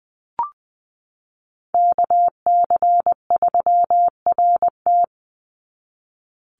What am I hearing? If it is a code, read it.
Morse code, heard as a keyed tone, about 20 wpm, 700 Hz. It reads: KC3RT
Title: NA SOTA Chaser Callsigns 20wpm